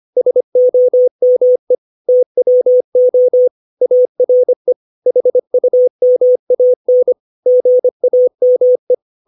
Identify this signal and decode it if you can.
Morse code, heard as a keyed tone, about 25 wpm, 500 Hz. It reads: SOME TWO ARE HUMAN GAME
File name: 632 words / 632 Words 25WPM